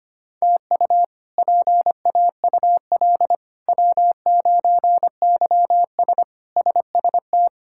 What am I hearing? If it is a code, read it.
Morse code, heard as a keyed tone, about 25 wpm, 700 Hz. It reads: TU PAUL W9YS SST
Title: K1USN SST QSO Practice - 25wpm